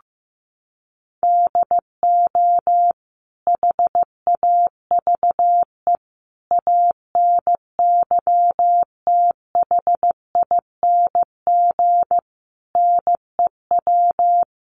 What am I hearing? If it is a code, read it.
Morse code, heard as a keyed tone, about 15 wpm, 700 Hz. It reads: DO HAVE ANYTHING NEW
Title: Sentences from Top 1300 Words 15wpm